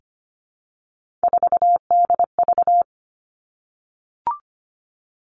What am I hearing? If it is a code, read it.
Morse code, heard as a keyed tone, about 25 wpm, 700 Hz. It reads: 4DV
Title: Three Challenging Letter-Number Combinations 25wpm